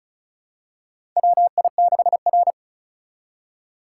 Morse code, heard as a keyed tone, about 34 wpm, 700 Hz. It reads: WI6R
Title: USA Call Signs - 4 Characters - 2x Character Spacing - 35wpm